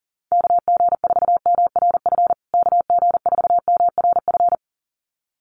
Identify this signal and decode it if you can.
Morse code, heard as a keyed tone, about 40 wpm, 700 Hz. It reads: KG4MRF KG4MRF
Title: USA Call Signs Repeated 2x 40wpm